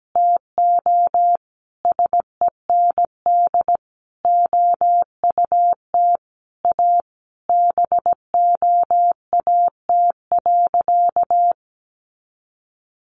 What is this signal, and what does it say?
Morse code, heard as a keyed tone, about 17 wpm, 700 Hz. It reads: TO SEND OUT A BOAT.
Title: Sentences from Top 1500 Words 17wpm